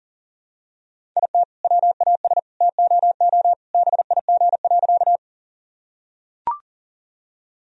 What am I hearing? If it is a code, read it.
Morse code, heard as a keyed tone, about 40 wpm, 700 Hz. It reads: IT WAS TOO BIG.
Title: Sentences from Top 1000 Words - Review Part 1 - 40wpm